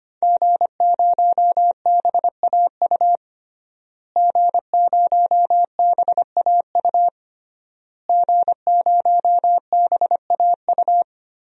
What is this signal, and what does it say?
Morse code, heard as a keyed tone, about 25 wpm, 700 Hz. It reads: G0BAU G0BAU G0BAU
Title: DXCC-100 Call Signs - 5 Characters - Repeated 3x - 25wpm